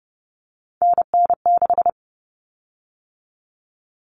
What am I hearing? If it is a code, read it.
Morse code, heard as a keyed tone, about 30 wpm, 700 Hz. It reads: NN6
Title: Common Cut Numbers - 3-Digit 30wpm